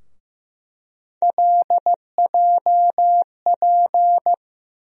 Morse code, heard as a keyed tone, about 16 wpm, 700 Hz. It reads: LJP
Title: USA Call Sign Postfixes - 3 Letters 15wpm